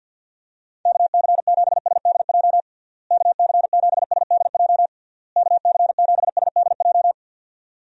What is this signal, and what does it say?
Morse code, heard as a keyed tone, about 50 wpm, 700 Hz. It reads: KC7SDJ KC7SDJ KC7SDJ
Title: USA Call Signs - 6 Characters - Repeated 3x - 50wpm